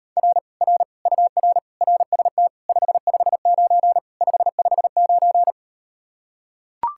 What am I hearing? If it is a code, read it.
Morse code, heard as a keyed tone, about 38 wpm, 700 Hz. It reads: R R UR RST 559 559